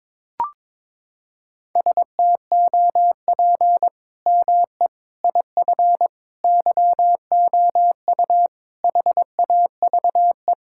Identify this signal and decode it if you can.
Morse code, heard as a keyed tone, about 21 wpm, 700 Hz. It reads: STOP ME IF YOU HAVE
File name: practice / 22wpm / Sentences with 8-Word Phrases - 22wpm